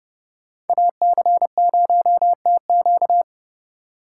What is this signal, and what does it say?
Morse code, heard as a keyed tone, about 30 wpm, 700 Hz. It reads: AC0TQ